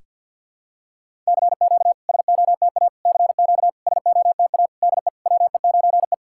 Morse code, heard as a keyed tone, about 51 wpm, 690 Hz. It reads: CQ SOTA CQ SOTA DE WE9E
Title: SOTA QSO Practice 50wpm